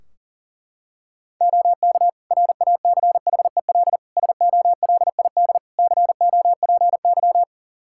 Morse code, heard as a keyed tone, about 40 wpm, 700 Hz. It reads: OK RACHEL SOLID COPY
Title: K1USN SST QSO Practice 40wpm